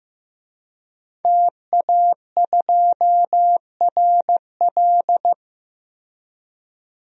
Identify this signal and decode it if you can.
Morse code, heard as a keyed tone, about 15 wpm, 700 Hz. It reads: TA2RL
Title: DXCC-100 Call Signs - 5 Characters - 15wpm